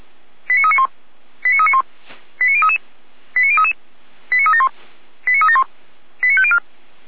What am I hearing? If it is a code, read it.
Five-tone selective call (ZVEI), heard as quick runs of five tones, tones of about 70 ms. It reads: 89391 89391 89033 89033 89371 89371 89504